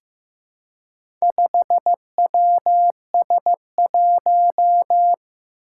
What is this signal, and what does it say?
Morse code, heard as a keyed tone, about 15 wpm, 700 Hz. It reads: 5WS1
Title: Four Challenging Letter-Number Combinations 15wpm